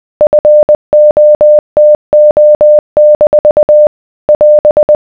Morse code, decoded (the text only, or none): FOTO- <AS>